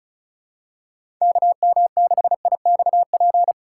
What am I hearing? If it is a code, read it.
Morse code, heard as a keyed tone, about 34 wpm, 690 Hz. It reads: KM6IXP
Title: USA Call Signs - 3x Character Spacing - 35wpm